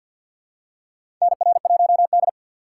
Morse code, heard as a keyed tone, about 52 wpm, 700 Hz. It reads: NR1D
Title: USA Call Signs - 4 Characters - 2x Character Spacing - 50wpm